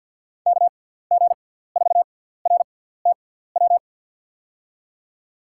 Morse code, heard as a keyed tone, about 53 wpm, spaced as for 17 wpm, 700 Hz. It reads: KG4RTW
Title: USA Call Signs - 6 Characters - 6x Character Spacing - 50wpm